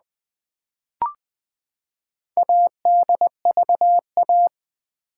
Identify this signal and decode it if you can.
Morse code, heard as a keyed tone, about 20 wpm, 700 Hz. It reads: ADVA
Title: Build A Word - 4 Letters At A Time - 8-Letter Words - 20wpm